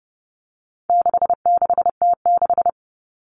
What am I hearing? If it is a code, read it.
Morse code, heard as a keyed tone, about 30 wpm, 700 Hz. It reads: 66T6
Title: Common Cut Numbers - 4-Digit 30wpm